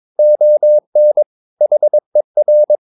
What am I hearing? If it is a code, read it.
Morse code, heard as a keyed tone, about 22 wpm, 600 Hz. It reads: ON HER
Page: Romanen Never af Ken Follet på engelsk.